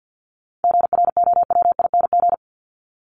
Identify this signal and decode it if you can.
Morse code, heard as a keyed tone, about 50 wpm, 700 Hz. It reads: GROWING
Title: ING Suffix - Top 100 Words with Base Words 50wpm